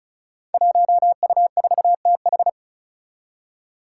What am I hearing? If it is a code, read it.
Morse code, heard as a keyed tone, about 36 wpm, 700 Hz. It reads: 1U4TH
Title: Five Challenging Letter-Number Combinations 35wpm